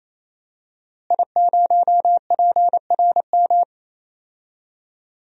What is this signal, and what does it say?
Morse code, heard as a keyed tone, about 28 wpm, 700 Hz. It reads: I0PRM
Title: DXCC-100 Call Signs - 5 Characters - Repeated 3x - 28wpm